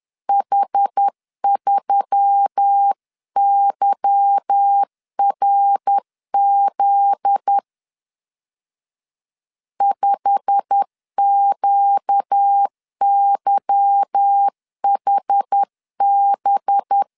Morse code, heard as a keyed tone, about 10 wpm, 800 Hz. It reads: H3YRZ 5QYHB
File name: LEZIONI / 10 SIGNS / 10C DOT 10wpm 2940